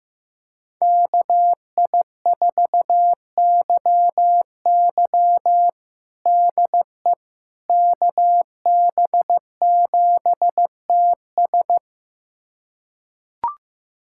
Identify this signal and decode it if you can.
Morse code, heard as a keyed tone, about 15 wpm, 700 Hz. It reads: KI4YY DE KB7TS
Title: USA Callsign DE Callsign - 5 Characters - 15wpm